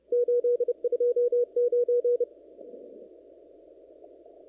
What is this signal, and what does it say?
Morse code, heard as a keyed tone, about 32 wpm, 500 Hz. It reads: 829
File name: CW / kt5j wpx cw-op-k5pi